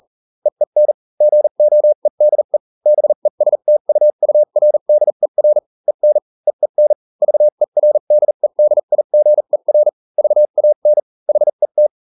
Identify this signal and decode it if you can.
Morse code, heard as a keyed tone, about 40 wpm, spaced as for 35 wpm, 600 Hz. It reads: EEN GOEDE BESTUURDER EN EEN VERDEDIGER VAN HET